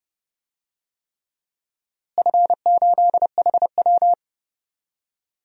Morse code, heard as a keyed tone, about 29 wpm, 700 Hz. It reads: F8HW
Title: DXCC-100 Call Signs - 4 Characters - Repeated 3x - 30wpm